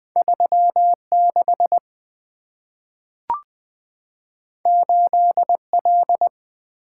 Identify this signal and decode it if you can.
Morse code, heard as a keyed tone, about 20 wpm, 700 Hz. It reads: 3 6 8 L
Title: Two Letter-Number Combinations - 20wpm